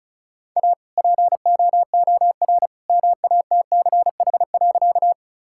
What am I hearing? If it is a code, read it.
Morse code, heard as a keyed tone, about 34 wpm, 700 Hz. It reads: A POOR MATCH.
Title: Sentences from Top 1600 Words 35wpm